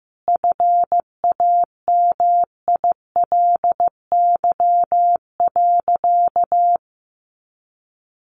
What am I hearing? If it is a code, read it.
Morse code, heard as a keyed tone, about 15 wpm, 700 Hz. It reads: FAMILY.
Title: Sentences from Top 1500 Words 15wpm